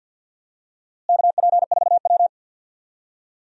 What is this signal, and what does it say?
Morse code, heard as a keyed tone, about 52 wpm, 700 Hz. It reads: KP4W
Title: DXCC-100 Call Signs - 4 Characters - Repeated 2x - 50wpm